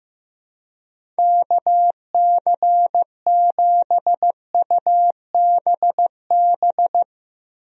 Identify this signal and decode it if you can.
Morse code, heard as a keyed tone, about 15 wpm, 700 Hz. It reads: KC7UBB
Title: USA Call Signs - 6 Characters - Repeated 2x - 15wpm